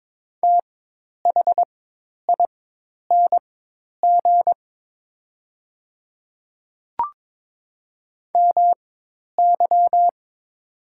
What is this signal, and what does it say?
Morse code, heard as a keyed tone, about 22 wpm, spaced as for 10 wpm, 700 Hz. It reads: THING MY
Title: 4x Character Spacing - Top 100 Words - 22wpm